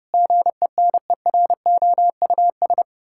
Morse code, heard as a keyed tone, about 28 wpm, 700 Hz. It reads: GENEROUS
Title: Pangram Challenge v2 - 30wpm